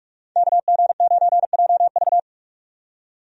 Morse code, heard as a keyed tone, about 45 wpm, 700 Hz. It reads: KG9JV
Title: USA Call Signs - 5 Characters - 45wpm